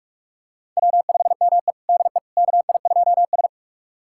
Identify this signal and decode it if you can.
Morse code, heard as a keyed tone, about 46 wpm, 700 Hz. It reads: W5ME DE KI2S
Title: USA Callsign DE Callsign - 4 Characters - 45wpm